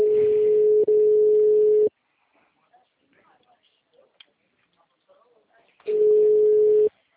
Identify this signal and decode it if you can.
Ringback tone (Europe, 425 Hz), with speech.
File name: out- -2002- - - .81241